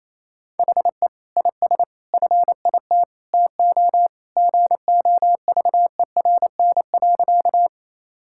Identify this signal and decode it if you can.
Morse code, heard as a keyed tone, about 28 wpm, 700 Hz. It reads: HE IS FIT TO GOVERN.
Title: Sentences from Top 1000 Words - Review Part 5 - 28wpm